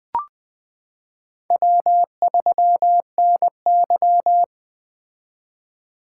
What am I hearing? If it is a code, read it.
Morse code, heard as a keyed tone, about 20 wpm, 700 Hz. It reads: W3NY
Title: 2023 POTA Activator Call Signs - 20wpm